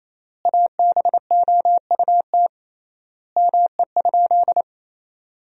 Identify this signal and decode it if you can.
Morse code, heard as a keyed tone, about 28 wpm, 700 Hz. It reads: ABOUT ME?
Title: 3x Word-Spacing - Sentences from Top 200 Words - 28wpm